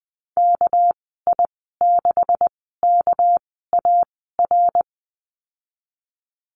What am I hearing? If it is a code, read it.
Morse code, heard as a keyed tone, about 20 wpm, spaced as for 15 wpm, 700 Hz. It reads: KI6KAR
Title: 200 Lesson 32g - Letter k - Call Signs 20wpm